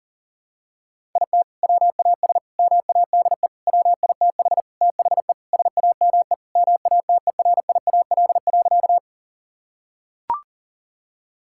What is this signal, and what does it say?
Morse code, heard as a keyed tone, about 40 wpm, 700 Hz. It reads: IT WAS MADE WITH THE SAME MATERIAL.
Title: Sentences from Top 1000 Words - Review Part 5 - 40wpm